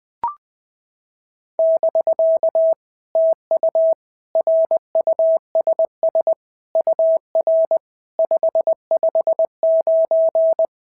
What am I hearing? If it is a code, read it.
Morse code, heard as a keyed tone, about 20 wpm, 650 Hz. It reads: <BK> TU RUSS UR 559
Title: POTA QSO Practice - 20wpm